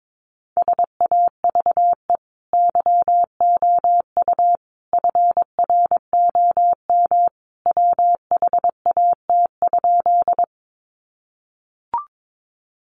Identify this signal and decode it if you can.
Morse code, heard as a keyed tone, about 22 wpm, 700 Hz. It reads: SAVE YOU FROM WHAT?
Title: Sentences from Top 1100 Words 22wpm